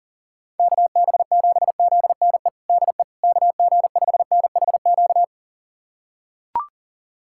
Morse code, heard as a keyed tone, about 40 wpm, 700 Hz. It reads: KB7ZNE DE KG5NHQ
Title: USA Callsign DE Callsign 40wpm